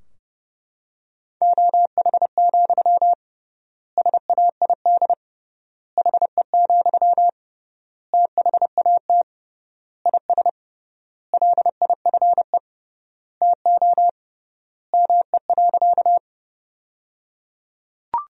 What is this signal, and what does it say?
Morse code, heard as a keyed tone, about 30 wpm, 700 Hz. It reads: OH, SAID HE, THAT IS LIFE TO ME.
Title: 3x Word-Spacing - Sentences from Top 500 Words - 30wpm